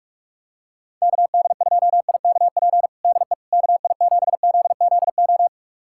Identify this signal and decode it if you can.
Morse code, heard as a keyed tone, about 46 wpm, 700 Hz. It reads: KD2IKP DE KI7ZZO
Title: USA Callsign DE Callsign - 6 Characters - 45wpm